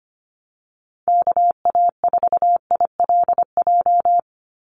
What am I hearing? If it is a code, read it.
Morse code, heard as a keyed tone, about 25 wpm, 700 Hz. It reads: KA4ILJ
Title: USA Call Signs 25wpm